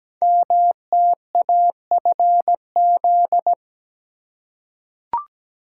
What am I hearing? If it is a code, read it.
Morse code, heard as a keyed tone, about 17 wpm, 700 Hz. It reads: MTAFZ
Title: Code Groups - 5 Letters - 17wpm